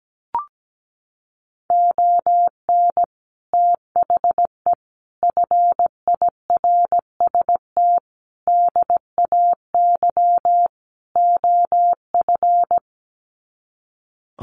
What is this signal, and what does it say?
Morse code, heard as a keyed tone, about 17 wpm, 700 Hz. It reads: ON THE FIRST DAY OF
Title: Sets of 5 Words Encore 17wpm